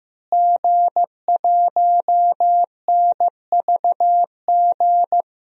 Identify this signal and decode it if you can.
Morse code, heard as a keyed tone, about 15 wpm, 700 Hz. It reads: G1NVG
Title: DXCC-100 Call Signs - 5 Characters - Repeated 2x - 15wpm